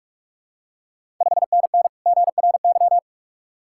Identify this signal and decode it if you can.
Morse code, heard as a keyed tone, about 41 wpm, 700 Hz. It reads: 5NN GRY
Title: Ontario QSO Party Exchange 45wpm